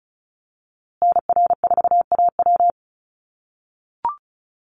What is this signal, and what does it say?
Morse code, heard as a keyed tone, about 34 wpm, 700 Hz. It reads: NR4AW
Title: USA Call Signs Repeated 3x 35wpm